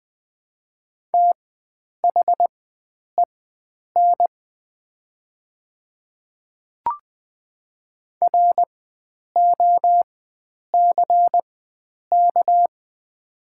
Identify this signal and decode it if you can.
Morse code, heard as a keyed tone, about 20 wpm, spaced as for 9 wpm, 700 Hz. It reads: THEN ROCK
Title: 4x Character Spacing - Top 500 Words - 20wpm